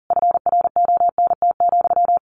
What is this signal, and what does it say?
Morse code, heard as a keyed tone, about 40 wpm, 700 Hz. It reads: FRONT,
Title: A Sound Of Thunder 40wpm